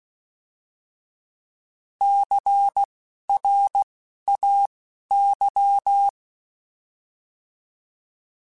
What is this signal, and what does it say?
Morse code, heard as a keyed tone, about 16 wpm, spaced as for 12 wpm, 800 Hz. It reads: CRAY
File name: Alphabetical / Alphabetical mp3s / Cs @10